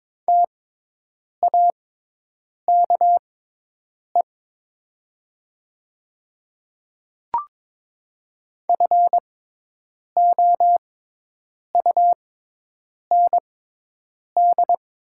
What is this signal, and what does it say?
Morse code, heard as a keyed tone, about 22 wpm, spaced as for 8 wpm, 700 Hz. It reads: TAKE FOUND
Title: 6x Character Spacing - Top 300 Words - 22wpm